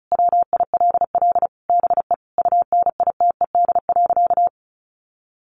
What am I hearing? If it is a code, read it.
Morse code, heard as a keyed tone, about 34 wpm, 700 Hz. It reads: WILL BE UNITED.
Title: Sentences from Top 600 Words 35wpm